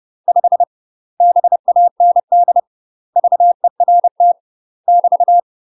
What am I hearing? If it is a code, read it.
Morse code, heard as a keyed tone, about 30 wpm, 700 Hz. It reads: 5 BAND VERT <BT>